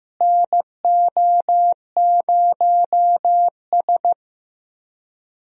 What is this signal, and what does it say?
Morse code, heard as a keyed tone, about 16 wpm, 700 Hz. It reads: NO0S
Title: USA Call Signs - 4 Characters - Repeated 3x - 15wpm